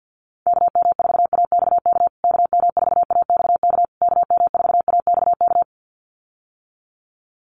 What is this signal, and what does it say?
Morse code, heard as a keyed tone, about 50 wpm, 700 Hz. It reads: KM4AXK KM4AXK KM4AXK
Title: USA Call Signs Repeated 3x 50wpm